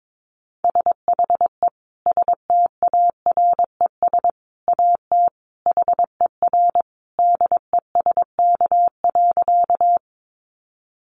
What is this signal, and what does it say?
Morse code, heard as a keyed tone, about 22 wpm, 700 Hz. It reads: SHE STARES AT HER DESK.
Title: Easy Phrases 22wpm